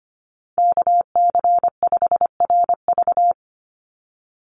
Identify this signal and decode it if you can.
Morse code, heard as a keyed tone, about 25 wpm, 700 Hz. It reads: KC5RV
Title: USA Call Signs 25wpm